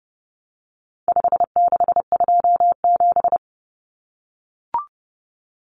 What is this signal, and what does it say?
Morse code, heard as a keyed tone, about 30 wpm, 700 Hz. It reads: 5627
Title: Common Cut Numbers - 4-Digit 30wpm